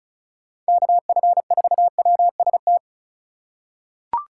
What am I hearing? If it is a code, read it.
Morse code, heard as a keyed tone, about 36 wpm, 700 Hz. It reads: KF4WST
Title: USA Call Signs - 6 Characters - 2x Character Spacing - 35wpm